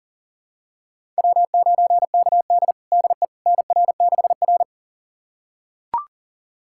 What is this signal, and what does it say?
Morse code, heard as a keyed tone, about 40 wpm, 700 Hz. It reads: W9KD DE NR6R
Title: USA Callsign DE Callsign - 4 Characters - 40wpm